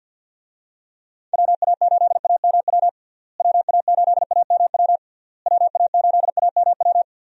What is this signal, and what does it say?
Morse code, heard as a keyed tone, about 53 wpm, 700 Hz. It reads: WA8AMW WA8AMW WA8AMW
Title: USA Call Signs - 6 Characters - Repeated 3x - 50wpm